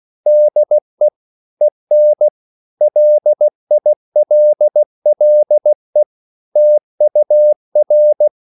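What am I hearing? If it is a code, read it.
Morse code, heard as a keyed tone, about 16 wpm, 600 Hz. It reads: DE EN LILLE TUR
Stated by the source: Slow (QRS) Morse code at 16 wpm, from CW med Gnister.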